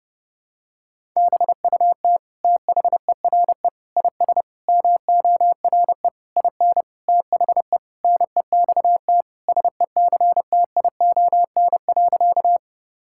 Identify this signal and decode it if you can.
Morse code, heard as a keyed tone, about 32 wpm, 700 Hz. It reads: BUT THERE IS MORE IN THE NEXT SECTION.
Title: Sentences from Top 1000 Words - Review Part 5 - 30wpm